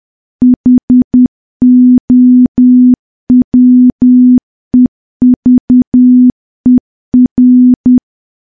Morse code, heard as a keyed tone, about 10 wpm, 250 Hz. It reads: HOWEVER
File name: morse / morse-training-10WPM